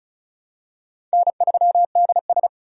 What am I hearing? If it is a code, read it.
Morse code, heard as a keyed tone, about 36 wpm, 700 Hz. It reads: N3DS
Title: USA Call Signs - 4 Characters - Repeated 2x - 35wpm